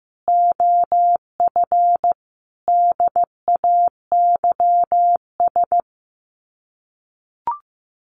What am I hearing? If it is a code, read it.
Morse code, heard as a keyed tone, about 15 wpm, 700 Hz. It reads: OF DAYS
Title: Sets of 5 Words Encore2 15wpm